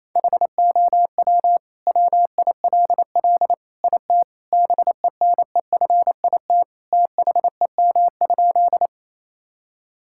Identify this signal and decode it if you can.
Morse code, heard as a keyed tone, about 28 wpm, 700 Hz. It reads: HOW WILL IT BENEFIT THEM?
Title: Sentences from Top 1000 Words - Review Part 6 - 28wpm